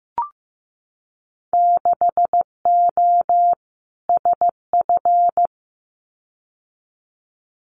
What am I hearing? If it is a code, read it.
Morse code, heard as a keyed tone, about 15 wpm, 700 Hz. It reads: 6O SF
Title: Winter Field Day Exchanges Encore 15wpm